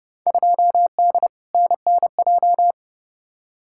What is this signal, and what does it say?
Morse code, heard as a keyed tone, about 30 wpm, 700 Hz. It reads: 2D NNJ
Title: Field Day Exchanges - 30wpm